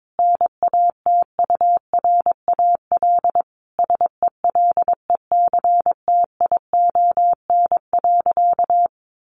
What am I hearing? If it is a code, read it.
Morse code, heard as a keyed tone, about 22 wpm, 700 Hz. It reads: NATURAL SELECTION.
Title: Sentences from Top 1400 Words 22wpm